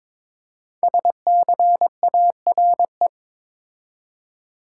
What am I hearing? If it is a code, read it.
Morse code, heard as a keyed tone, about 22 wpm, 700 Hz. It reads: SCARE
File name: 5 Letter Words - First 3 Letters Given - 22wpm